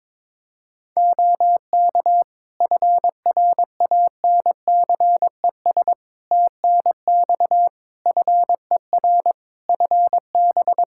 Morse code, heard as a keyed tone, about 22 wpm, 700 Hz. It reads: OK FRANCES TNX FER FB
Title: Practice QSO - Exchange 4 - Ending - 22wpm